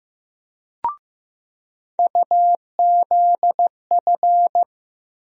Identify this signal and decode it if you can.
Morse code, heard as a keyed tone, about 15 wpm, 700 Hz. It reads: UZF
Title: USA Call Sign Postfixes - 3 Letters 15wpm